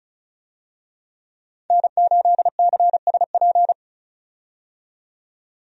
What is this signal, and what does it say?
Morse code, heard as a keyed tone, about 36 wpm, 700 Hz. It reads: N8CSP
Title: USA Call Signs - 5 Characters - 35wpm